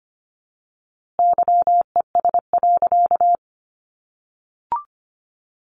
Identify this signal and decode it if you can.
Morse code, heard as a keyed tone, about 25 wpm, 700 Hz. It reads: YES.
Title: A Sound Of Thunder 25wpm